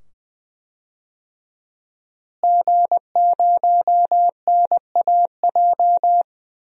Morse code, heard as a keyed tone, about 20 wpm, 700 Hz. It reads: G0NAJ
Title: DXCC-100 Call Signs - 5 Characters - Repeated 2x - 20wpm